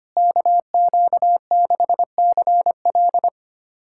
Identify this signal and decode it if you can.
Morse code, heard as a keyed tone, about 25 wpm, 700 Hz. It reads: KQ6CL
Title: USA Callsign DE Callsign - 5 Characters - 25wpm